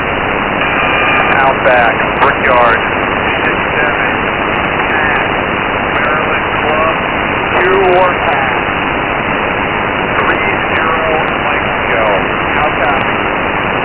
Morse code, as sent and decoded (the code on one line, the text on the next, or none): none